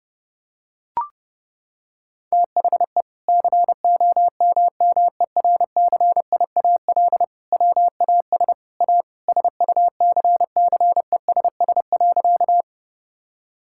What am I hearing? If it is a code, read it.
Morse code, heard as a keyed tone, about 30 wpm, 700 Hz. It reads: THE COMMERCIAL WAS A SUCCESS.
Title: Sentences from Top 1000 Words - Review Part 5 - 30wpm